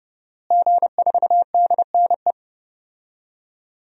Morse code, heard as a keyed tone, about 30 wpm, 700 Hz. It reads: G4DNE
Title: DXCC-100 Call Signs - 5 Characters - Repeated 3x - 30wpm